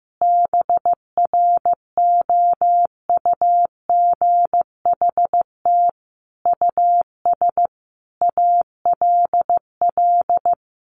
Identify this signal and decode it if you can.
Morse code, heard as a keyed tone, about 15 wpm, 700 Hz. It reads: BROUGHT US ALL
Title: Sentences from Top 1500 Words 15wpm